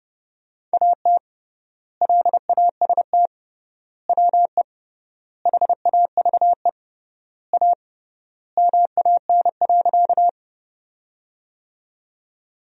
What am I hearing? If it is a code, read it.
Morse code, heard as a keyed tone, about 32 wpm, 700 Hz. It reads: AT LAST WE HAVE A MAN.
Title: 3x Word-Spacing - Sentences from Top 300 Words - 30wpm